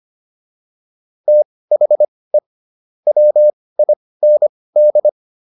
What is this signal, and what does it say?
Morse code, heard as a keyed tone, about 25 wpm, spaced as for 18 wpm, 600 Hz. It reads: THE WIND